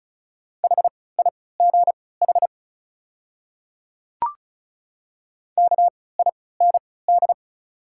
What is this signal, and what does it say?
Morse code, heard as a keyed tone, about 36 wpm, spaced as for 20 wpm, 700 Hz. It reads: HIGH KIND
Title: 3x Character Spacing - Top 200 Words - 35wpm